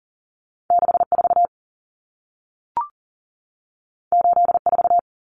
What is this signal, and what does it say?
Morse code, heard as a keyed tone, about 40 wpm, 700 Hz. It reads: 64 84
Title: Numbers - 2-Digit 40wpm